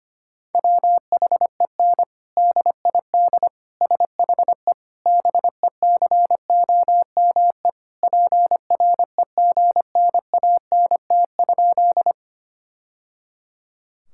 Morse code, heard as a keyed tone, about 25 wpm, 700 Hz. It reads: WHEN DID SHE BECOME PREGNANT?
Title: Sentences from Top 1000 Words - Review Part 6 - 25wpm